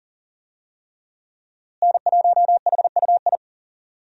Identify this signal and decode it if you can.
Morse code, heard as a keyed tone, about 40 wpm, 700 Hz. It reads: N1HUI
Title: USA Call Signs - 5 Characters - Repeated 3x - 40wpm